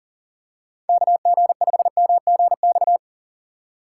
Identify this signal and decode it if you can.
Morse code, heard as a keyed tone, about 40 wpm, 700 Hz. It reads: KG5MGX
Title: USA Call Signs - 3x Character Spacing - 40wpm